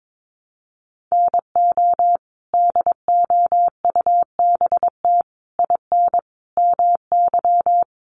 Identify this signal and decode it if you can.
Morse code, heard as a keyed tone, about 22 wpm, 700 Hz. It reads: NO DOUBT IN MY
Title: Sets of 5 Words Encore2 22wpm